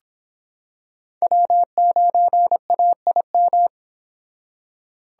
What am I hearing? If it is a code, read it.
Morse code, heard as a keyed tone, about 25 wpm, 700 Hz. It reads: W9AIM